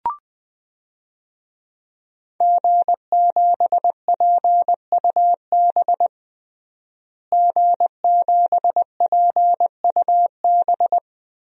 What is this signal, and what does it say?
Morse code, heard as a keyed tone, about 20 wpm, 700 Hz. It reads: G7PUB G7PUB
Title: DXCC-100 Call Signs - 5 Characters - Repeated 3x - 20wpm